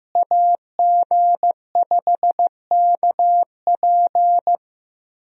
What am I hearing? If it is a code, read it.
Morse code, heard as a keyed tone, about 15 wpm, 700 Hz. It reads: AG5KP
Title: USA Call Signs - 5 Characters - Repeated 2x - 15wpm